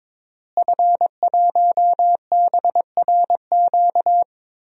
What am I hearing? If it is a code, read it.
Morse code, heard as a keyed tone, about 21 wpm, 700 Hz. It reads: F1BRQ
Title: DXCC-100 Call Signs - 5 Characters - Repeated 3x - 22wpm